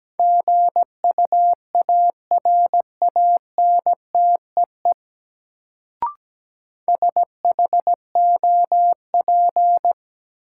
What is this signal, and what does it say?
Morse code, heard as a keyed tone, about 17 wpm, 700 Hz. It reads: GUARANTEE SHOP
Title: Sentences from Top 1600 Words - New Word Drill Alt - 17wpm